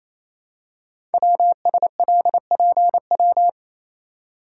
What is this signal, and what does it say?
Morse code, heard as a keyed tone, about 29 wpm, 700 Hz. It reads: WSLPW
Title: Code Groups - 5 Letters - 28wpm